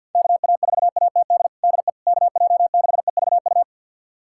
Morse code, heard as a keyed tone, about 53 wpm, 700 Hz. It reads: KA4ATD DE KJ6EVU
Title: USA Callsign DE Callsign - 6 Characters - 50wpm